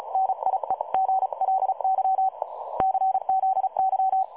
Morse code, heard as a keyed tone, about 36 wpm, 760 Hz. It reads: NIEGLY CZY
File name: dlazaawansowanych / nagraniazpasma / websdr recording 3565.0kHz